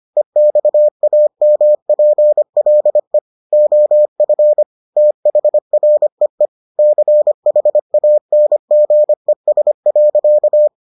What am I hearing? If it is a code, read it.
Morse code, heard as a keyed tone, about 25 wpm, 600 Hz. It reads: EXAMPLE OF THREE CHANGES.